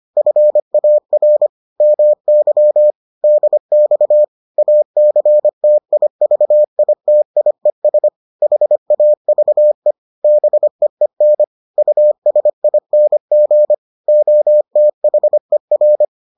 Morse code, heard as a keyed tone, about 25 wpm, 600 Hz. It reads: FAR MY DX ACTIVITIES HAVE BEEN USING OTHER